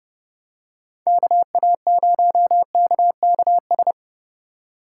Morse code, heard as a keyed tone, about 31 wpm, 700 Hz. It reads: KA0KKS
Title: USA Call Signs - 6 Characters - 2x Character Spacing - 30wpm